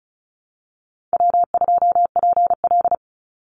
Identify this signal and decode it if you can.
Morse code, heard as a keyed tone, about 34 wpm, 700 Hz. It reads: W2PL
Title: Call Signs Repeated 3x - 35wpm